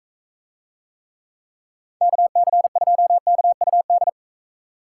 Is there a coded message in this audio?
Yes: Morse code from 2.0 to 4.1 s, heard as a keyed tone, about 43 wpm, 700 Hz. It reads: KC2KUD